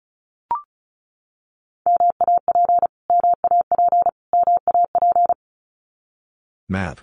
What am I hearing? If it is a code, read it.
Morse code, heard as a keyed tone, about 34 wpm, 700 Hz. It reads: MAP MAP MAP
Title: Top 500 Words Repeated 35wpm